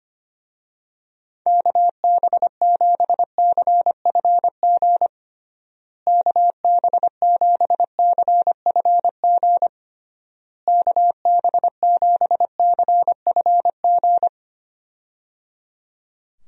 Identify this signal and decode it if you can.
Morse code, heard as a keyed tone, about 25 wpm, 700 Hz. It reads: KB7CFG KB7CFG KB7CFG
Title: USA Call Signs - 6 Characters - Repeated 3x - 25wpm